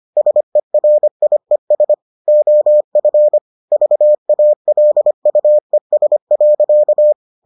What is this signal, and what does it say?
Morse code, heard as a keyed tone, about 25 wpm, 600 Hz. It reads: SERIES OF VALUES.